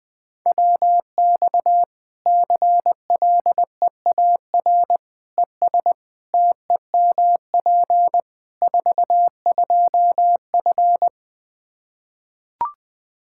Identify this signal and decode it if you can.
Morse code, heard as a keyed tone, about 21 wpm, 700 Hz. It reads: WX CLEAR ES TEMP 42F
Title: Practice QSO - Exchange 2 - Info - 20wpm